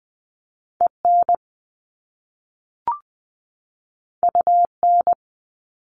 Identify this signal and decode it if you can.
Morse code, heard as a keyed tone, about 20 wpm, 700 Hz. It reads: EN UN
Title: Top Two Letter Combinations in Words 20wpm